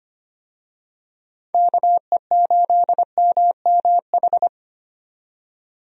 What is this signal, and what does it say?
Morse code, heard as a keyed tone, about 25 wpm, 700 Hz. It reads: KE8MMH
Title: USA Call Signs - 6 Characters - 25wpm